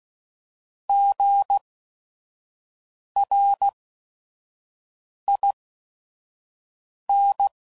Morse code, heard as a keyed tone, about 16 wpm, spaced as for 5 wpm, 800 Hz. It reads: GRIN
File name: Prefixes / By First Letter / Gs